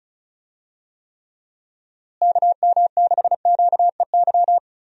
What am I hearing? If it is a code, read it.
Morse code, heard as a keyed tone, about 36 wpm, 700 Hz. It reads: KM6QEY
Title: USA Call Signs - 6 Characters - Repeated 2x - 35wpm